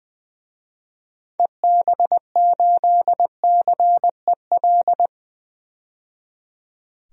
Morse code, heard as a keyed tone, about 20 wpm, 700 Hz. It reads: EB8CEL
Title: DXCC-100 Call Signs - 6 Characters - 20wpm